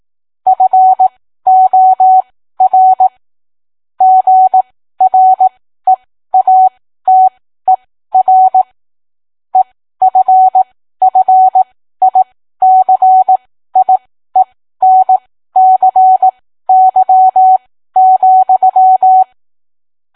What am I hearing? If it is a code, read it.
Morse code, heard as a keyed tone, about 18 wpm, spaced as for 13 wpm, 750 Hz. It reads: FOR GREATER EFFICIENCY,